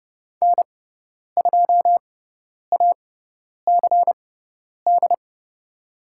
Morse code, heard as a keyed tone, about 30 wpm, spaced as for 10 wpm, 700 Hz. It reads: N2ACD
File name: Lesson 16g - Letter c - Call Signs 30wpm